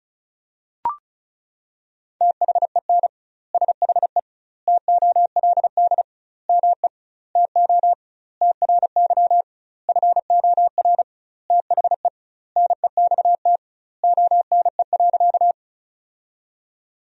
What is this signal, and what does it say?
Morse code, heard as a keyed tone, about 34 wpm, 700 Hz. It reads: THEN SHE TOLD ME TO TRY FOR THE NEXT ONE.
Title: 2x Word-Spacing - Sentences from Top 400 Words - 35wpm